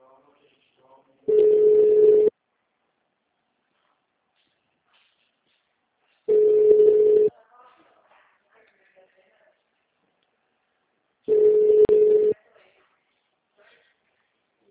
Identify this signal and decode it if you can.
Ringback tone (Europe, 425 Hz), with speech.
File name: out- -2004- - - .40